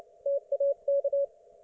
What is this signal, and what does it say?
Morse code, heard as a keyed tone, about 29 wpm, 560 Hz. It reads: TAK